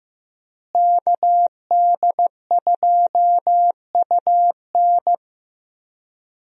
Morse code, heard as a keyed tone, about 15 wpm, 700 Hz. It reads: KD2UN
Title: USA Call Signs - 5 Characters - Repeated 3x - 15wpm